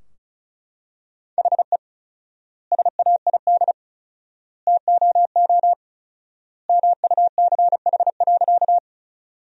Morse code, heard as a keyed tone, about 34 wpm, 700 Hz. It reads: HE SAID TOO MUCH.
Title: 4x Word-Spacing - Sentences from Top 200 Words - 35wpm